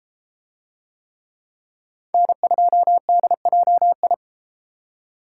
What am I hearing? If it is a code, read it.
Morse code, heard as a keyed tone, about 33 wpm, 700 Hz. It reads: N2DJI